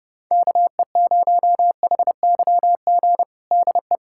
Morse code, heard as a keyed tone, about 30 wpm, 700 Hz. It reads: KE0HYG DE
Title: USA Callsign DE Callsign - 6 Characters - 30wpm